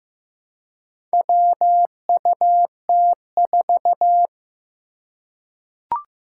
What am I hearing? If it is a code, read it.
Morse code, heard as a keyed tone, about 15 wpm, 700 Hz. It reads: WUT4
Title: Four Challenging Letter-Number Combinations 15wpm